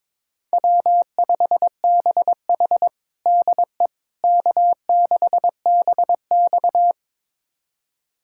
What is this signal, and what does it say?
Morse code, heard as a keyed tone, about 22 wpm, 700 Hz. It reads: W5BH DE K6BX
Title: USA Callsign DE Callsign - 4 Characters - 22wpm